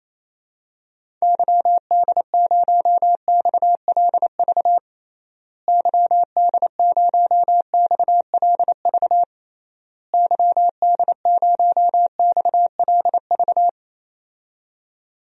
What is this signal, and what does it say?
Morse code, heard as a keyed tone, about 28 wpm, 700 Hz. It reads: YD0XLV YD0XLV YD0XLV
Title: DXCC-100 Call Signs - 6 Characters - Repeated 3x - 28wpm